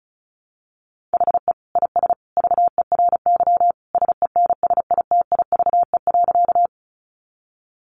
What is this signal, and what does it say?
Morse code, heard as a keyed tone, about 34 wpm, 700 Hz. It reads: HE IS VERY SENSITIVE.
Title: Sentences from Top 1400 Words 35wpm